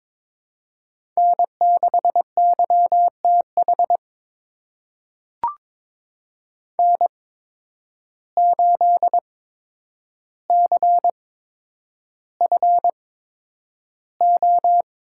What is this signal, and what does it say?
Morse code, heard as a keyed tone, about 22 wpm, spaced as for 9 wpm, 700 Hz. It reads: N6YTH N8CFO
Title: USA Call Signs - 5 Characters - 8x Character Spacing - 22wpm